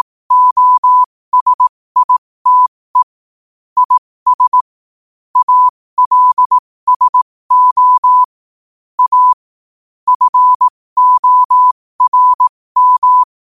Morse code, heard as a keyed tone, about 18 wpm, spaced as for 15 wpm, 1000 Hz. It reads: OSITE IS ALSO A FORM
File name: qotdfiles / 15wpm